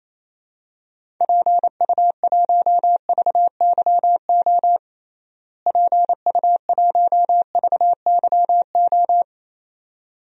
Morse code, heard as a keyed tone, about 28 wpm, 700 Hz. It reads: PU1VYO PU1VYO
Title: DXCC-100 Call Signs - 6 Characters - Repeated 2x - 28wpm